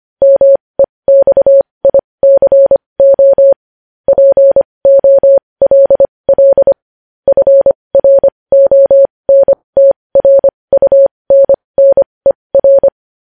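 Morse code, heard as a keyed tone, about 25 wpm, spaced as for 20 wpm, 550 Hz. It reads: MEXICO POLL FRONTRUNNER
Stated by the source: Morse code news headlines sent at 20.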